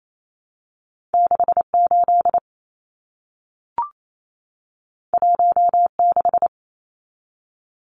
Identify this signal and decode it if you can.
Morse code, heard as a keyed tone, about 28 wpm, 700 Hz. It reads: 68 16
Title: Numbers - 2-Digit 28wpm